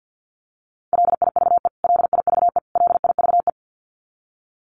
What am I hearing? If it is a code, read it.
Morse code, heard as a keyed tone, about 50 wpm, 700 Hz. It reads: LIVE LIVE LIVE
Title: Top 500 Words Repeated 50wpm